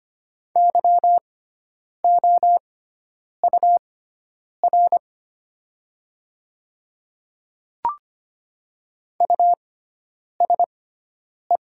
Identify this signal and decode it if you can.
Morse code, heard as a keyed tone, about 25 wpm, spaced as for 9 wpm, 700 Hz. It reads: YOUR USE
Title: 6x Character Spacing - Top 100 Words - 25wpm